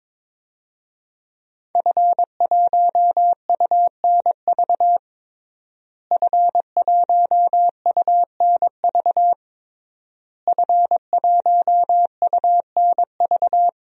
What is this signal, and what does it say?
Morse code, heard as a keyed tone, about 21 wpm, 700 Hz. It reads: F1UNV F1UNV F1UNV
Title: DXCC-100 Call Signs - 5 Characters - Repeated 3x - 22wpm